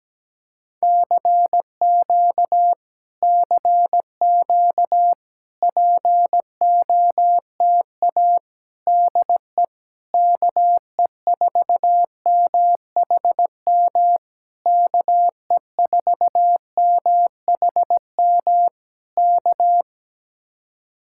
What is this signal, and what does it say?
Morse code, heard as a keyed tone, about 17 wpm, 700 Hz. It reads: CQ CQ POTA DE KE4MHM KE4MHM K
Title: Speed-Racing Common QSO Phrases - 17wpm